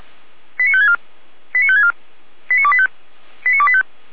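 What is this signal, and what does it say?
Five-tone selective call (ZVEI), heard as quick runs of five tones, tones of about 70 ms. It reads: 89564 89564 89285 89285
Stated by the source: ZVEI five-tone selective call received on SDR.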